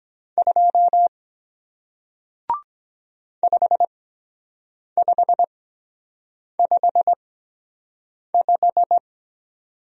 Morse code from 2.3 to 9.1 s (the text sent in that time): 5555